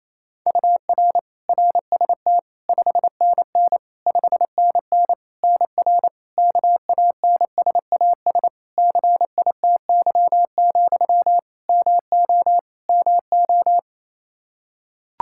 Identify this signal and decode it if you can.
Morse code, heard as a keyed tone, about 28 wpm, 700 Hz. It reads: UR RST 5NN 5NN NR KANSAS CITY, MO MO
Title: Practice QSO - Exchange 1 - Beginning - 28wpm